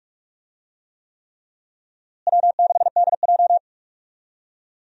Morse code, heard as a keyed tone, about 46 wpm, 700 Hz. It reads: W6DJ
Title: USA Call Signs - 4 Characters - Repeated 3x - 45wpm